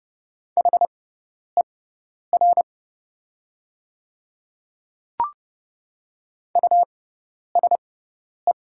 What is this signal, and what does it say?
Morse code, heard as a keyed tone, about 30 wpm, spaced as for 10 wpm, 690 Hz. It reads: HER USE
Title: 6x Character Spacing - Top 100 Words - 30wpm